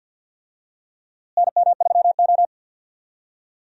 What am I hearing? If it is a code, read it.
Morse code, heard as a keyed tone, about 50 wpm, 700 Hz. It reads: NM3O
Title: USA Call Signs - 4 Characters - Repeated 3x - 50wpm